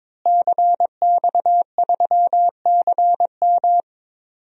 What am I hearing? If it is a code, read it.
Morse code, heard as a keyed tone, about 22 wpm, 700 Hz. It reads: CX3CM
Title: DXCC-100 Call Signs - 5 Characters - Repeated 3x - 22wpm